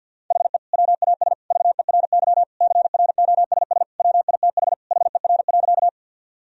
Morse code, heard as a keyed tone, about 50 wpm, 700 Hz. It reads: HE WAS VERY CROSS WITH HER.
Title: Sentences from Top 1000 Words - Review Part 1 - 50wpm